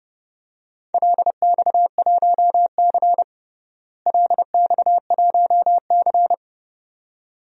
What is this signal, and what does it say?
Morse code, heard as a keyed tone, about 30 wpm, 700 Hz. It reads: LX1C LX1C
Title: DXCC-100 Call Signs - 4 Characters - Repeated 2x - 30wpm